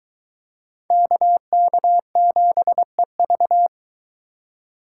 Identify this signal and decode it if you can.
Morse code, heard as a keyed tone, about 22 wpm, 700 Hz. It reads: KK7EV